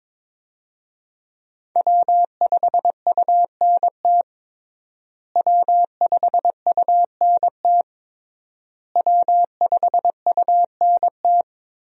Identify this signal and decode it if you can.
Morse code, heard as a keyed tone, about 22 wpm, 700 Hz. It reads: W5UNT W5UNT W5UNT
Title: USA Call Signs - 5 Characters - Repeated 3x - 22wpm